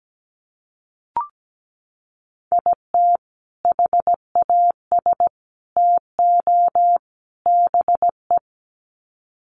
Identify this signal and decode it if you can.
Morse code, heard as a keyed tone, about 17 wpm, 700 Hz. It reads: IT HAS TO BE
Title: Sets of 4 Words 17wpm